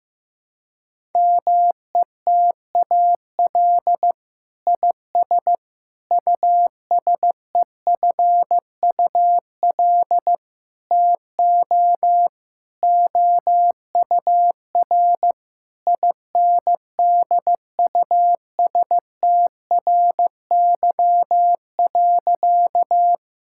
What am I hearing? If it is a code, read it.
Morse code, heard as a keyed tone, about 15 wpm, 700 Hz. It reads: METAL IS USEFUL TO OUR INDUSTRY.
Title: Sentences from Top 1000 Words - Review Part 6 - 15wpm